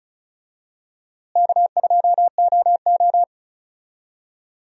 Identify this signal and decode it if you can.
Morse code, heard as a keyed tone, about 36 wpm, 700 Hz. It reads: K2OO
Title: USA Call Signs - 4 Characters - Repeated 2x - 35wpm